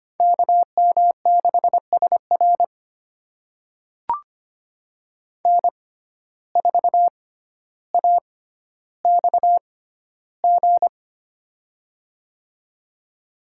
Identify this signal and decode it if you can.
Morse code, heard as a keyed tone, about 25 wpm, spaced as for 13 wpm, 700 Hz. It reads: KM6SR N4AXG
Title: USA Call Signs - 5 Characters - 6x Character Spacing - 25wpm